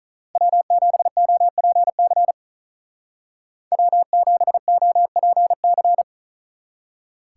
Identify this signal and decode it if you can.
Morse code, heard as a keyed tone, about 47 wpm, 700 Hz. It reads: W7OPC W7OPC